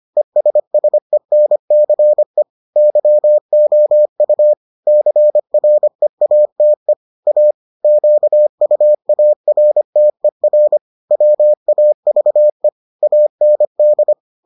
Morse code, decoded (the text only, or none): ESSENCE YOU CREATE A QUARTER WAVE AND